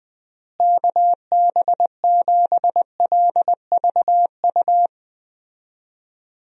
Morse code, heard as a keyed tone, about 20 wpm, 700 Hz. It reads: KB7LVU
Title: USA Call Signs - 6 Characters - Repeated 3x - 20wpm